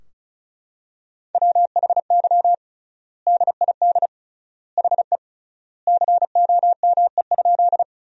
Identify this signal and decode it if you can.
Morse code, heard as a keyed tone, about 34 wpm, 700 Hz. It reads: WHY DID HE COME?
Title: 3x Word-Spacing - Sentences from Top 200 Words - 35wpm